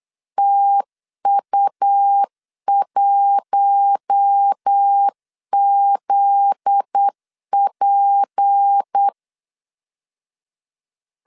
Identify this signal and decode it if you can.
Morse code, heard as a keyed tone, about 8 wpm, 800 Hz. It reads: TU1ZP
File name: LEZIONI / 8 YJ45 / 8 YJ4 8wpm 3246